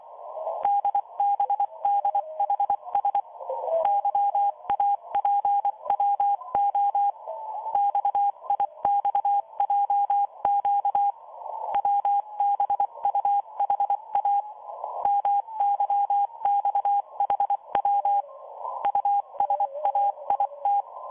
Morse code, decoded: DBDHS YAPWO XIXJQ WBUHA MYXHW USAIT